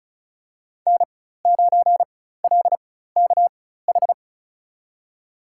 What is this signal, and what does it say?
Morse code, heard as a keyed tone, about 36 wpm, spaced as for 16 wpm, 700 Hz. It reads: N9LKH
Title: USA Call Signs - 5 Characters - 4x Character Spacing - 35wpm